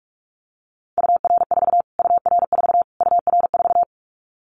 Morse code, heard as a keyed tone, about 44 wpm, 700 Hz. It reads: UR4 UR4 UR4
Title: Callsign Prefixes for Top 25 DXCC - Repeated 3x - 45wpm